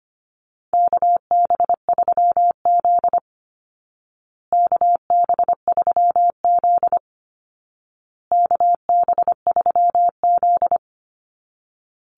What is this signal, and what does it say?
Morse code, heard as a keyed tone, about 25 wpm, 700 Hz. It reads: KB3Z KB3Z KB3Z
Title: Call Signs Repeated 2x - 25wpm